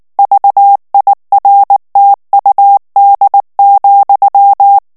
Morse code, heard as a keyed tone, about 19 wpm, 800 Hz. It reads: VIRTUD,
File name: morse mp3 / ESPAGNOL TEXTE EN CLAIR 2 VITESSE 18 MOTS MINUTES